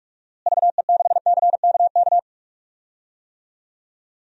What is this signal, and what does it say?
Morse code, heard as a keyed tone, about 46 wpm, 700 Hz. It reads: VE6CKK
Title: DXCC-100 Call Signs - 6 Characters - Repeated 3x - 45wpm